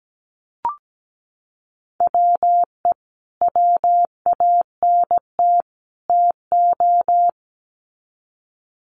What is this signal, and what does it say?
Morse code, heard as a keyed tone, about 17 wpm, 700 Hz. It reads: WE WANT TO
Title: Sets of 3 Words 17wpm